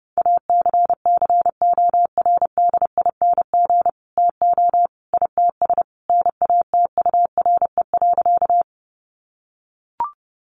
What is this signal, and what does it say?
Morse code, heard as a keyed tone, about 30 wpm, 700 Hz. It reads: ACCORDING TO ITS NATURE.
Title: Sentences from Top 700 Words 30wpm